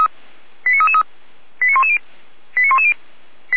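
Five-tone selective call (ZVEI), heard as quick runs of five tones, tones of about 70 ms. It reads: 89333 89110 89110
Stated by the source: Five-tone ZVEI selective call received by SDR, fire brigade alerting (Feuerwehr Orlamünde).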